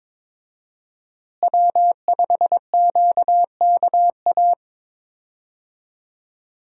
Morse code, heard as a keyed tone, about 21 wpm, 700 Hz. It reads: W5QKA
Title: USA Call Signs - 5 Characters - Repeated 2x - 22wpm